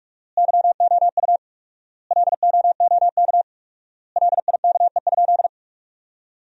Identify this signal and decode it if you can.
Morse code, heard as a keyed tone, about 46 wpm, 700 Hz. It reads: YOU LOOK LIKE?
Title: 4x Word-Spacing - Sentences from Top 100 Words - 45wpm